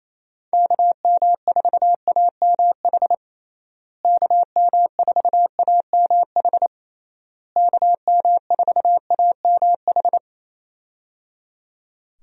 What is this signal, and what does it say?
Morse code, heard as a keyed tone, about 28 wpm, 700 Hz. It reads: KM4AMH KM4AMH KM4AMH
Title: USA Call Signs - 6 Characters - Repeated 3x - 28wpm